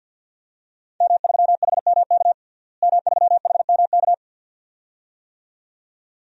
Morse code, heard as a keyed tone, about 52 wpm, 700 Hz. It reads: M3HMK M3HMK
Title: DXCC-100 Call Signs - 5 Characters - Repeated 2x - 50wpm